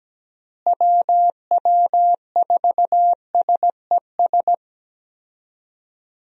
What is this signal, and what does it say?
Morse code, heard as a keyed tone, about 17 wpm, 700 Hz. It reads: WW4SES
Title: USA Call Signs - 6 Characters - Repeated 2x - 17wpm